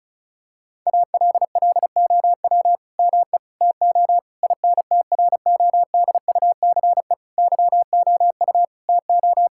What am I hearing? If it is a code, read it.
Morse code, heard as a keyed tone, about 36 wpm, 700 Hz. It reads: ALLOW ME TO INTRODUCE YOU TO
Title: Sentences from Top 1600 Words 35wpm